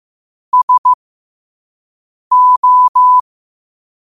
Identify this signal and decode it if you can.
Morse code, heard as a keyed tone, about 15 wpm, spaced as for 5 wpm, 1000 Hz. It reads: SO
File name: qotdfiles / 5wpm